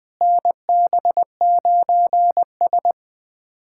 Morse code, heard as a keyed tone, about 20 wpm, 700 Hz. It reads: NB9S
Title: USA Callsign DE Callsign - 4 Characters - 20wpm